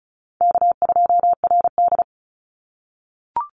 Morse code, heard as a keyed tone, about 36 wpm, 700 Hz. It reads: K2RD
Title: Call Signs - 35wpm